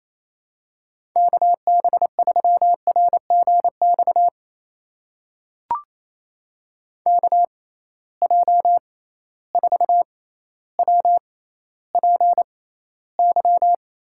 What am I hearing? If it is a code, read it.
Morse code, heard as a keyed tone, about 28 wpm, spaced as for 14 wpm, 700 Hz. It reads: KB3RGX KJ4WPY
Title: USA Call Signs - 6 Characters - 6x Character Spacing - 28wpm